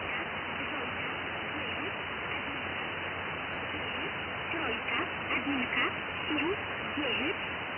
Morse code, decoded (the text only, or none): none